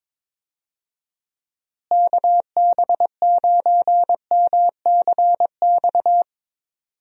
Morse code, heard as a keyed tone, about 22 wpm, 700 Hz. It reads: KB9MCX
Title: USA Call Signs - 6 Characters - Repeated 3x - 22wpm